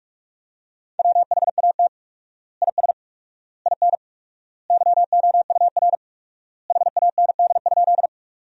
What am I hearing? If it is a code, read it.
Morse code, heard as a keyed tone, about 41 wpm, 700 Hz. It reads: WHAT IS IN YOUR HAND?
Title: 4x Word-Spacing - Sentences from Top 200 Words - 45wpm